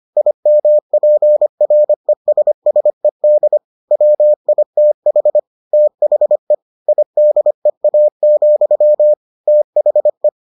Morse code, heard as a keyed tone, about 25 wpm, 600 Hz. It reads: IMPRESSED WITH THE IDEA, THE